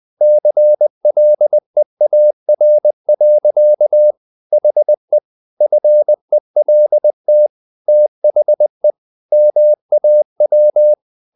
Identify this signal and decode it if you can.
Morse code, heard as a keyed tone, about 20 wpm, 600 Hz. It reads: CLEAR. HE FELT THE MAW
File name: the old man and the sea part 2 20wpm